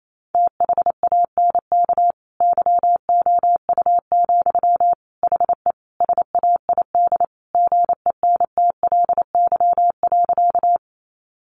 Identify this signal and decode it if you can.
Morse code, heard as a keyed tone, about 28 wpm, 700 Hz. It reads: THANK YOU, HE SAID GENTLY.
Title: Sentences from Top 1500 Words 28wpm